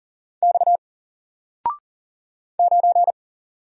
Morse code, heard as a keyed tone, about 38 wpm, 700 Hz. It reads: X 9
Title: Single Letter-Number 40wpm